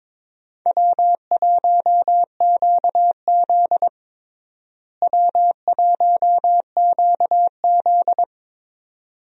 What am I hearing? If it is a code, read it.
Morse code, heard as a keyed tone, about 21 wpm, 700 Hz. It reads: W1QZ W1QZ
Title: USA Call Signs - 4 Characters - Repeated 2x - 22wpm